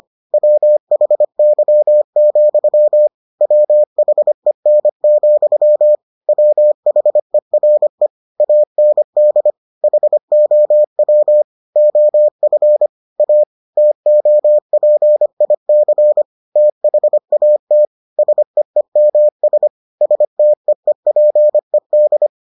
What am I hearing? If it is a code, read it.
Morse code, heard as a keyed tone, about 25 wpm, 600 Hz. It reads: WHY, WHEN, WHERE AND HOW OF A TOPIC THAT SEEMS STEEPED